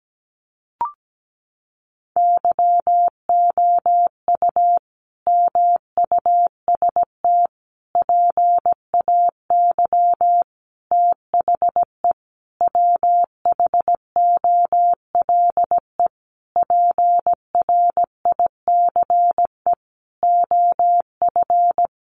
Morse code, heard as a keyed tone, about 17 wpm, 700 Hz. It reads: YOU MUST PAY THE WHOLE PRICE OF
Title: Sentences from Top 1400 Words 17wpm